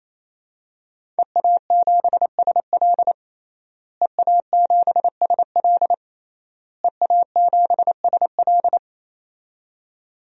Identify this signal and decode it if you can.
Morse code, heard as a keyed tone, about 28 wpm, 700 Hz. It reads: EA7SL EA7SL EA7SL
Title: DXCC-100 Call Signs - 5 Characters - Repeated 3x - 28wpm